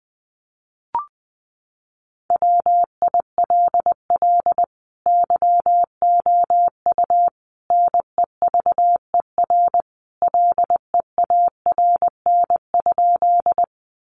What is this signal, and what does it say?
Morse code, heard as a keyed tone, about 20 wpm, 700 Hz. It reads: WILL YOU NEVER LEARN?
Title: Sentences from Top 300 Words 20wpm